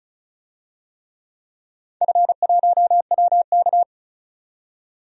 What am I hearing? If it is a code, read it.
Morse code, heard as a keyed tone, about 34 wpm, 700 Hz. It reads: F1WK
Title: DXCC-100 Call Signs - 4 Characters - 35wpm